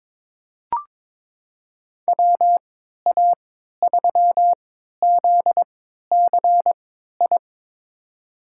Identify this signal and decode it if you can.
Morse code, heard as a keyed tone, about 22 wpm, spaced as for 13 wpm, 700 Hz. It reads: WA3ZCI
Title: USA Call Signs - 6 Characters - 3x Character Spacing - 22wpm